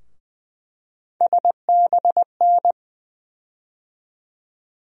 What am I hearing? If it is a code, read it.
Morse code, heard as a keyed tone, about 20 wpm, 700 Hz. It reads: SBN
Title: USA Call Sign Postfixes - 3 Letters 20wpm